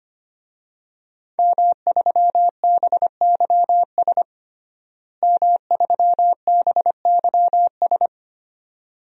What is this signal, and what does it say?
Morse code, heard as a keyed tone, about 25 wpm, 700 Hz. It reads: M3BYS M3BYS
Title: DXCC-100 Call Signs - 5 Characters - Repeated 2x - 25wpm